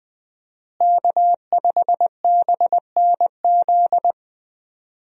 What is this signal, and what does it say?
Morse code, heard as a keyed tone, about 20 wpm, 700 Hz. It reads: K5BNZ
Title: USA Call Signs - 5 Characters - Repeated 2x - 20wpm